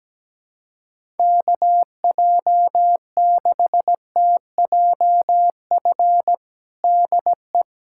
Morse code, heard as a keyed tone, about 17 wpm, 700 Hz. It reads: KJ6TJF DE
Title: USA Callsign DE Callsign - 6 Characters - 17wpm